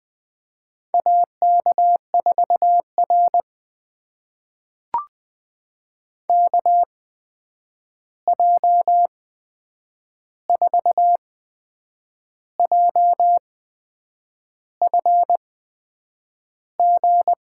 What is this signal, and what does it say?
Morse code, heard as a keyed tone, about 20 wpm, spaced as for 8 wpm, 700 Hz. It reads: AK4R KJ4JFG
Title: USA Call Signs - 8x Character Spacing - 20wpm